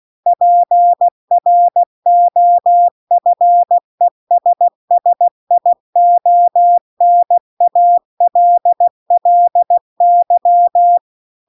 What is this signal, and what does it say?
Morse code, heard as a keyed tone, about 16 wpm, 700 Hz. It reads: PROFESSIONALLY
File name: CW-The-Big-Twoheaded-River-2-16WPM